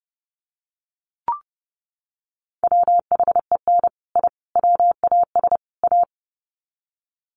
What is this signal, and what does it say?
Morse code, heard as a keyed tone, about 30 wpm, 700 Hz. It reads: WHEN I WAS A
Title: Sets of 4 Words 30wpm